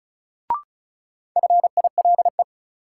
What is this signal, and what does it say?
Morse code, heard as a keyed tone, about 34 wpm, 700 Hz. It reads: FILE
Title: Sentences from Top 1000 Words - Review Part 4 - New Word Drill - 35wpm